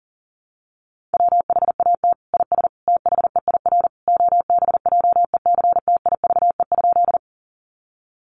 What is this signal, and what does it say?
Morse code, heard as a keyed tone, about 37 wpm, 700 Hz. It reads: WHAT IS THEIR OBJECTIVE?
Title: Sentences from Top 1200 Words 40wpm